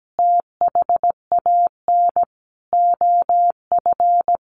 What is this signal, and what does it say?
Morse code, heard as a keyed tone, about 17 wpm, 700 Hz. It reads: THAN OF
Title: Sentences from Top 1400 Words 17wpm